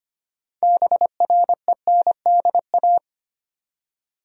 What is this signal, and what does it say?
Morse code, heard as a keyed tone, about 25 wpm, 700 Hz. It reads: BRENDA
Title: USA Common Names - 25wpm